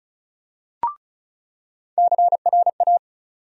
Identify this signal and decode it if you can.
Morse code, heard as a keyed tone, about 34 wpm, 700 Hz. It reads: CRA
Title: USA Call Sign Postfixes - 3 Letters 35wpm